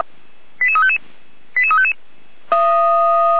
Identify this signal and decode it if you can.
Five-tone selective call (ZVEI), heard as quick runs of five tones, tones of about 80 ms. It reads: 88355 88355
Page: /455/SDR/ZVEI/88191/